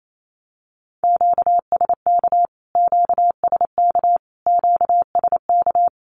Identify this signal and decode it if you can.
Morse code, heard as a keyed tone, about 28 wpm, 700 Hz. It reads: QSK QSK QSK
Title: Q-Codes Repeated 3x 28wpm